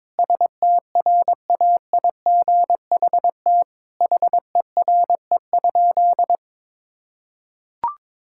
Morse code, heard as a keyed tone, about 22 wpm, 700 Hz. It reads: STRAIGHT HERE?
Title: Sentences from Top 1000 Words - Review Part 6 - 22wpm